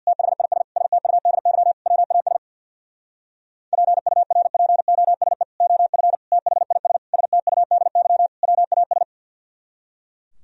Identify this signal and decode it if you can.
Morse code, heard as a keyed tone, about 64 wpm, 700 Hz. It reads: THIS STUDY WAS PURPOSE OF THIS STUDY WAS